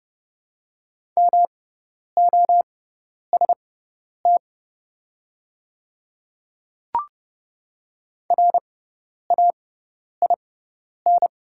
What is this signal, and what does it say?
Morse code, heard as a keyed tone, about 30 wpm, spaced as for 10 wpm, 690 Hz. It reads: MOST RAIN
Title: 6x Character Spacing - Top 500 Words - 30wpm